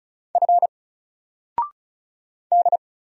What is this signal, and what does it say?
Morse code, heard as a keyed tone, about 36 wpm, 700 Hz. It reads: F D
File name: Single-Letter-Number-35wpm